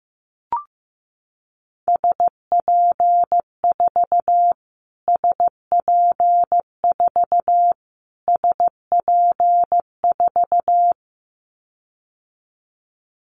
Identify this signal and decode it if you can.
Morse code, heard as a keyed tone, about 15 wpm, 700 Hz. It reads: SP4 SP4 SP4
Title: Callsign Prefixes for Top 25 DXCC - Repeated 3x - 15wpm